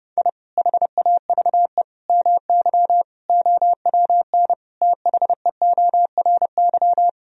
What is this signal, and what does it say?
Morse code, heard as a keyed tone, about 32 wpm, 700 Hz. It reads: I HAVE MY OWN THEORY
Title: Sentences from Top 1000 Words - Review Part 3 - 30wpm